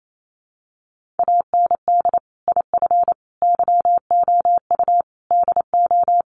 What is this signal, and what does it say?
Morse code, heard as a keyed tone, about 28 wpm, 700 Hz. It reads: AND IF YOU DO
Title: Sentences from Top 1500 Words 28wpm